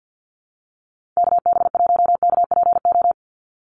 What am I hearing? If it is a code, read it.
Morse code, heard as a keyed tone, about 50 wpm, 700 Hz. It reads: KB1KPO
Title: USA Call Signs 50wpm